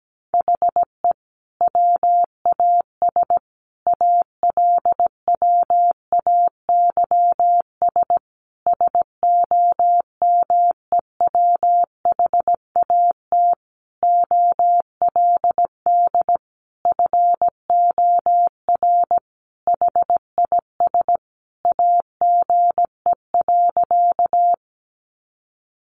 Morse code, heard as a keyed tone, about 17 wpm, 700 Hz. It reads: HE WAS ALWAYS SOMEWHAT OLD FOR HIS AGE.
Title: Sentences from Top 1300 Words 17wpm